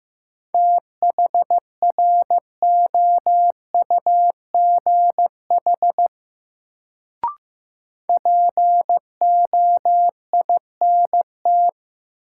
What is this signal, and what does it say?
Morse code, heard as a keyed tone, about 15 wpm, 700 Hz. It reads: THROUGH POINT
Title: Sentences from Top 200 Words - New Word Drill - 15wpm